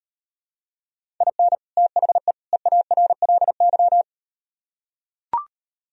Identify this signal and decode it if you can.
Morse code, heard as a keyed tone, about 38 wpm, 700 Hz. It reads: IN THE EARLY